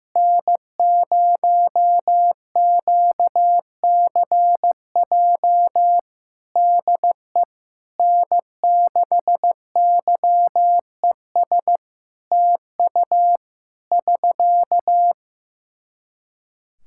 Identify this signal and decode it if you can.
Morse code, heard as a keyed tone, about 15 wpm, 700 Hz. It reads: N0QCJ DE N6YES TU <SK>
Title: K1USN SST QSO Practice 15wpm